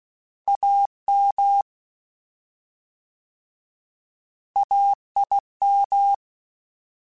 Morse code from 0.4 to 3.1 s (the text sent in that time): AM